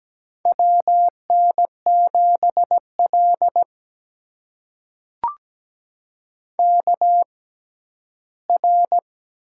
Morse code, heard as a keyed tone, about 17 wpm, spaced as for 11 wpm, 700 Hz. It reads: WN7L KR
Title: USA Call Signs - 4 Characters - 6x Character Spacing - 17wpm